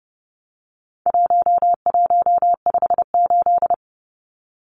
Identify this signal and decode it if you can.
Morse code, heard as a keyed tone, about 30 wpm, 700 Hz. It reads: 1158
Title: Numbers - 4-Digit 30wpm